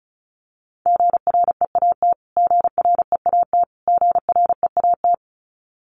Morse code, heard as a keyed tone, about 34 wpm, 700 Hz. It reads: GREAT GREAT GREAT
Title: Top 200 Words Repeated 35wpm